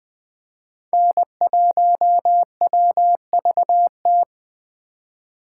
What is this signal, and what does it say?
Morse code, heard as a keyed tone, about 21 wpm, 700 Hz. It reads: N1WVT